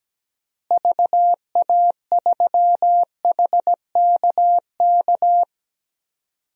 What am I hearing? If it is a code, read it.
Morse code, heard as a keyed tone, about 17 wpm, 700 Hz. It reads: VA3HKK
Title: DXCC-100 Call Signs - 6 Characters - Repeated 2x - 17wpm